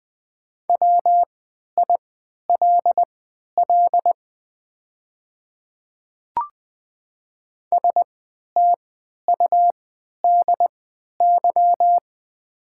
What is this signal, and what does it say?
Morse code, heard as a keyed tone, about 20 wpm, spaced as for 11 wpm, 690 Hz. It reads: WILL STUDY
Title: 3x Character Spacing - Top 500 Words - 20wpm